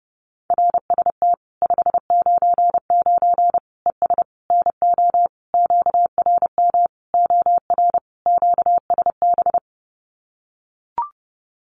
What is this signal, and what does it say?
Morse code, heard as a keyed tone, about 30 wpm, 700 Hz. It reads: RST 599 ES NO QRM OR QSB
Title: Common QSO Phrases 30wpm